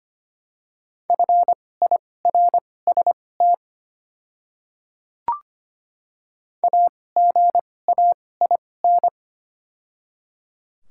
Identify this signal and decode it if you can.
Morse code, heard as a keyed tone, about 25 wpm, spaced as for 18 wpm, 700 Hz. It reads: FIRST AGAIN
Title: 2x Character Spacing - Top 500 Words - 25wpm